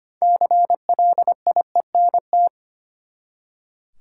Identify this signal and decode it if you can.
Morse code, heard as a keyed tone, about 25 wpm, 700 Hz. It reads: CLIENT
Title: Six Letter Words - 25wpm